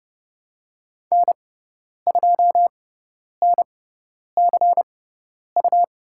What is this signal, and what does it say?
Morse code, heard as a keyed tone, about 31 wpm, spaced as for 10 wpm, 700 Hz. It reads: N2NCU
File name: Lesson 18g - Letter m - Call Signs 30wpm